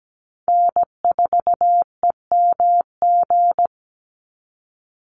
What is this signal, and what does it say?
Morse code, heard as a keyed tone, about 17 wpm, 700 Hz. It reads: N4EMG
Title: NA SOTA Chaser Callsigns 17wpm